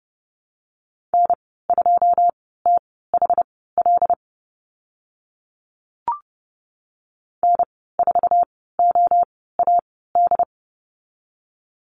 Morse code, heard as a keyed tone, about 30 wpm, spaced as for 17 wpm, 700 Hz. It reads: N2THL N4OAD
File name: Lesson 15g - Number 5 - Call Signs 30wpm